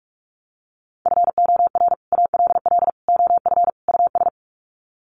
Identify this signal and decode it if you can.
Morse code, heard as a keyed tone, about 44 wpm, 700 Hz. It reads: FOR ALL OF US
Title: Sets of 4 Words 45wpm